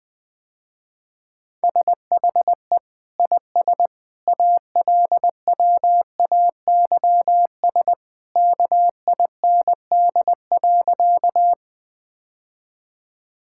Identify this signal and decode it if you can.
Morse code, heard as a keyed tone, about 20 wpm, 700 Hz. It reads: SHE IS ALWAYS KIND.
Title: Sentences with 2-Word Phrases - Spoken Beginning - 20wpm